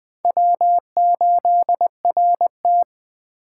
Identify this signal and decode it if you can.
Morse code, heard as a keyed tone, about 21 wpm, 700 Hz. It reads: W8RT
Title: Speed Racing - USA 1x2 Call Signs 17wpm